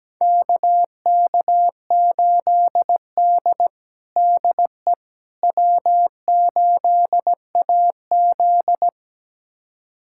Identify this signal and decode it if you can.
Morse code, heard as a keyed tone, about 17 wpm, 700 Hz. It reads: KK8D DE W8AZ
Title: USA Callsign DE Callsign - 4 Characters - 17wpm